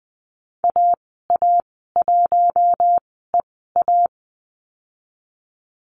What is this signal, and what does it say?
Morse code, heard as a keyed tone, about 20 wpm, spaced as for 15 wpm, 700 Hz. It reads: AA1EA
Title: 042 Lesson 8g - Number 1 - Call Signs 20wpm